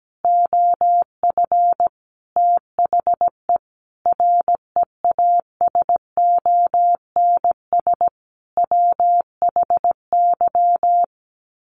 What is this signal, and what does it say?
Morse code, heard as a keyed tone, about 17 wpm, 700 Hz. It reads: OF THE REASONS WHY
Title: Sets of 5 Words 17wpm